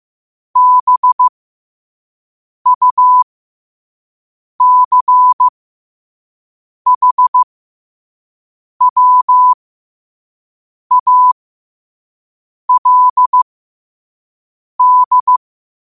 Morse code, heard as a keyed tone, about 15 wpm, spaced as for 5 wpm, 1000 Hz. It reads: BUCHWALD
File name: 5wpm